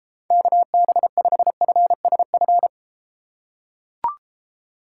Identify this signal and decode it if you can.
Morse code, heard as a keyed tone, about 33 wpm, 700 Hz. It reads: KB5FSF